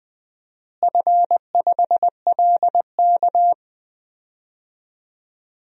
Morse code, heard as a keyed tone, about 20 wpm, 700 Hz. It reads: F5LK
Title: DXCC-100 Call Signs - 4 Characters - Repeated 3x - 20wpm